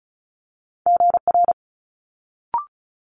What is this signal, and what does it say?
Morse code, heard as a keyed tone, about 36 wpm, 700 Hz. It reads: GR
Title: Multi-letter Phonemes Quick Fire 35wpm